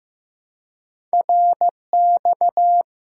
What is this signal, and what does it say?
Morse code, heard as a keyed tone, about 15 wpm, 700 Hz. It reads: RX
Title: Top 90 QSO Elements 15wpm